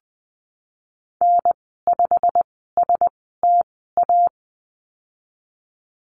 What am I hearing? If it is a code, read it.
Morse code, heard as a keyed tone, about 20 wpm, spaced as for 15 wpm, 700 Hz. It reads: N5STA
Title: 088 Lesson 15g - Number 5 - Call Signs 20wpm